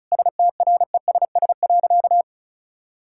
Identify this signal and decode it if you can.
Morse code, heard as a keyed tone, about 36 wpm, 700 Hz. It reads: STRESS.
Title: Sentences from Top 1000 Words - Review Part 6 - 35wpm